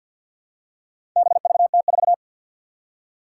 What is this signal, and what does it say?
Morse code, heard as a keyed tone, about 50 wpm, 700 Hz. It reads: BVT4
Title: Four Challenging Letter-Number Combinations 50wpm